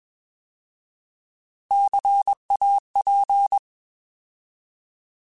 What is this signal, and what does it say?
Morse code, heard as a keyed tone, about 21 wpm, 800 Hz. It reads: CAP
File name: Alphabetical / Alphabetical mp3s / Cs @20